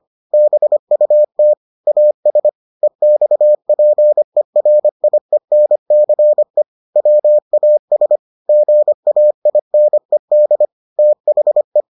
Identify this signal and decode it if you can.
Morse code, heard as a keyed tone, about 25 wpm, 600 Hz. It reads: BUT AS EXPERIENCE WAS GAINED THE